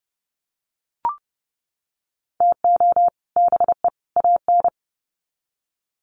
Morse code, heard as a keyed tone, about 30 wpm, 700 Hz. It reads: TO BE AN
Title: Sets of 3 Words 30wpm